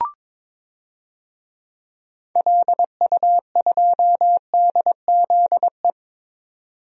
Morse code, heard as a keyed tone, about 22 wpm, 700 Hz. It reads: LU2DZE
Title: DXCC-100 Call Signs - 6 Characters - 22wpm